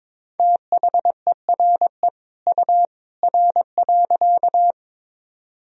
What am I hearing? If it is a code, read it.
Morse code, heard as a keyed tone, about 22 wpm, 700 Hz. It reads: THERE U R.
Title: Sentences from Top 100 Words and Abbreviations 22wpm